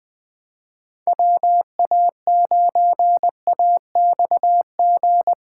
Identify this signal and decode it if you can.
Morse code, heard as a keyed tone, about 20 wpm, 700 Hz. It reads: WA9AXG
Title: Speed Racing - USA Call Signs 17wpm